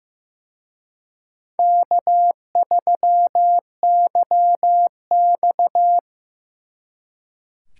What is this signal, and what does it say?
Morse code, heard as a keyed tone, about 18 wpm, 700 Hz. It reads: K3YX
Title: Speed Racing - USA 1x2 Call Signs 15wpm